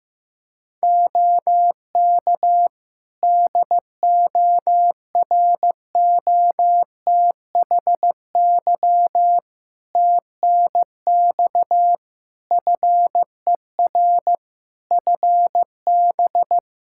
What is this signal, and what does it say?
Morse code, heard as a keyed tone, about 15 wpm, 700 Hz. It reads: OK DOROTHY TNX FER FB
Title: Practice QSO - Exchange 4 - Ending - 15wpm